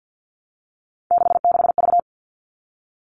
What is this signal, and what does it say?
Morse code, heard as a keyed tone, about 50 wpm, 700 Hz. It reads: 66V
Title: All Cut Numbers - 3-Digit 50wpm